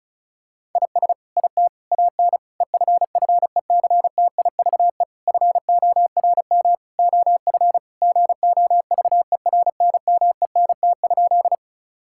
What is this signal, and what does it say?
Morse code, heard as a keyed tone, about 36 wpm, 700 Hz. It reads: IS IT AN EFFECTIVE FORM OF GOVERNMENT?
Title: Sentences from Top 1000 Words - Review Part 6 - 35wpm